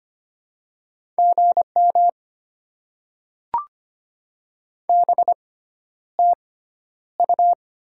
Morse code, heard as a keyed tone, about 25 wpm, spaced as for 11 wpm, 690 Hz. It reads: GM BTU
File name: Top 20 QSO Elements - 6x Character Spacing - 25wpm